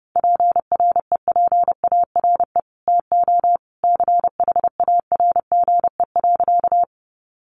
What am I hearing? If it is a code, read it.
Morse code, heard as a keyed tone, about 30 wpm, 700 Hz. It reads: PREPARE TO CHARGE.
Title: Sentences from Top 900 Words 30wpm